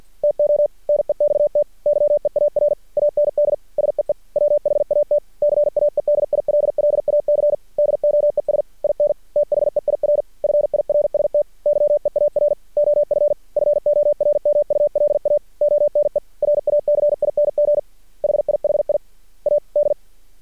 Morse code, heard as a keyed tone, about 53 wpm, 590 Hz. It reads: TO NEXT YEAR AND SEE WHAT CAEDILLAC DOES IN THEIR FIRST YEAR OF FORMULA ONE RACING HIHI A D
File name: archived / CRSnet